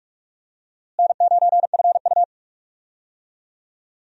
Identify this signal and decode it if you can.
Morse code, heard as a keyed tone, about 46 wpm, 700 Hz. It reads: N9FU
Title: USA Call Signs - 4 Characters - Repeated 3x - 45wpm